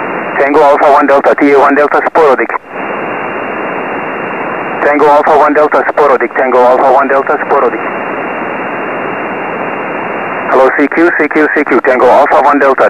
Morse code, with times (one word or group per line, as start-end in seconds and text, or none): none